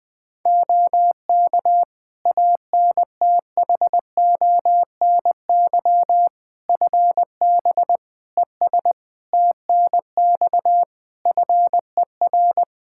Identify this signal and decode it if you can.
Morse code, heard as a keyed tone, about 20 wpm, 700 Hz. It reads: OK ANTHONY FB ES TNX FER
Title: Practice QSO - Exchange 2 - Info - 20wpm